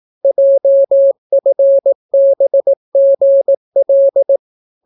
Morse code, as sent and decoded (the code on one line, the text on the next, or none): .--- ..-. -... --. .-..
JFBGL